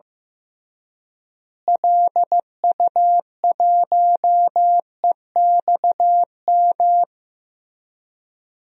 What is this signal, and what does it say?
Morse code, heard as a keyed tone, about 15 wpm, 700 Hz. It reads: LU1EXM
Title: DXCC-100 Call Signs - 6 Characters - Repeated 3x - 15wpm